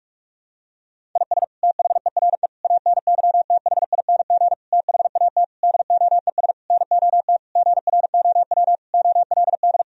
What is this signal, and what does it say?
Morse code, heard as a keyed tone, about 46 wpm, 700 Hz. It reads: IS THERE ANYTHING THAT DOES NOT GROW OLD
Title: Sentences from Top 1000 Words - Review Part 1 - 45wpm